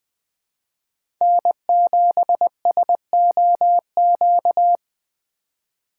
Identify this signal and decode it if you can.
Morse code, heard as a keyed tone, about 21 wpm, 700 Hz. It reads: N7SOQ
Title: USA Call Signs - 5 Characters - 3x Character Spacing - 20wpm